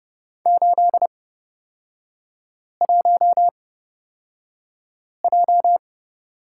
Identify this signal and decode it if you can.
Morse code, heard as a keyed tone, about 30 wpm, spaced as for 5 wpm, 700 Hz. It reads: 81J
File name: Lesson 38e - Letter q - Three Characters 30wpm